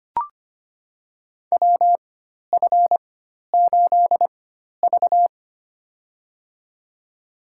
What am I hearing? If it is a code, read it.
Morse code, heard as a keyed tone, about 25 wpm, spaced as for 12 wpm, 700 Hz. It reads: WF8V
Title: USA Call Signs - 4 Characters - 4x Character Spacing - 25wpm